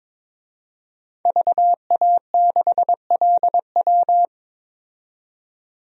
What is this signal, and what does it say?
Morse code, heard as a keyed tone, about 22 wpm, 700 Hz. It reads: VA6LW
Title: DXCC-100 Call Signs - 5 Characters - 22wpm